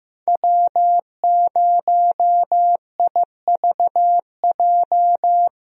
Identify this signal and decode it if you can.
Morse code, heard as a keyed tone, about 15 wpm, 700 Hz. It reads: W0IVJ
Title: USA Call Signs - 5 Characters - Repeated 2x - 15wpm